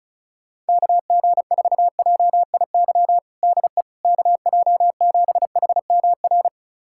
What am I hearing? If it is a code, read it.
Morse code, heard as a keyed tone, about 36 wpm, 700 Hz. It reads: KG4JIY DE KJ7HMR
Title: USA Callsign DE Callsign 35wpm